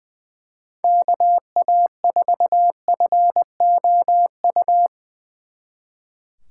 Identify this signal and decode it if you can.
Morse code, heard as a keyed tone, about 23 wpm, 700 Hz. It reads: KA4FOU
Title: Speed Racing - USA 2x3 Call Signs 20wpm